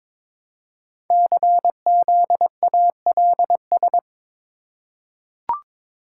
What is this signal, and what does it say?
Morse code, heard as a keyed tone, about 22 wpm, 700 Hz. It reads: CZALS
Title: Code Groups - 5 Letters - 22wpm